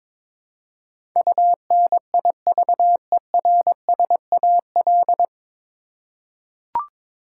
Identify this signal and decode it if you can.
Morse code, heard as a keyed tone, about 22 wpm, 700 Hz. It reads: UNIVERSAL
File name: 9 Letter Words - First 3 Letters Given - 22wpm